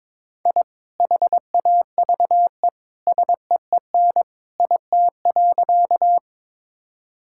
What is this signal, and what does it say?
Morse code, heard as a keyed tone, about 22 wpm, 700 Hz. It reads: I HAVE SEEN IT.
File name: practice / 22wpm / Sentences with 2-Word Phrases - 22wpm